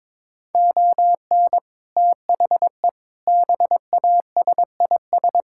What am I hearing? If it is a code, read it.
Morse code, heard as a keyed tone, about 22 wpm, 700 Hz. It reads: ON THE BASIS
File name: practice / 22wpm / Build A Phrase - 5-Word Phrases - 22wpm